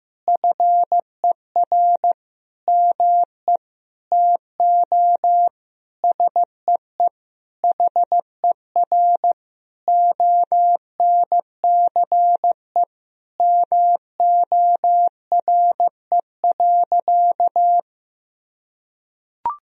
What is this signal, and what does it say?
Morse code, heard as a keyed tone, about 15 wpm, 700 Hz. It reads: FER ME TO SEE HER ONCE MORE.
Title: Sentences from Top 500 Words and Abbreviations 15wpm